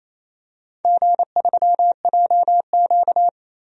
Morse code, heard as a keyed tone, about 28 wpm, 700 Hz. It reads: G3JQ
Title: DXCC-100 Call Signs - 4 Characters - 28wpm